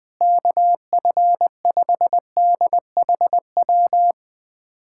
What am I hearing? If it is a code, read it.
Morse code, heard as a keyed tone, about 20 wpm, 700 Hz. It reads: KF5DHW
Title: USA Callsign DE Callsign - 6 Characters - 20wpm